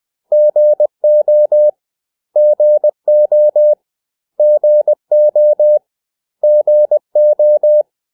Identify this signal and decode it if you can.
Morse code, heard as a keyed tone, about 20 wpm, 600 Hz. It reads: GO GO GO GO
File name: cw / 100 Common English Words